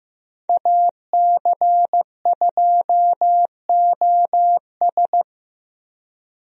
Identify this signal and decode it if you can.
Morse code, heard as a keyed tone, about 15 wpm, 700 Hz. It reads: AC2OS
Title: USA Call Signs - 5 Characters - Repeated 2x - 15wpm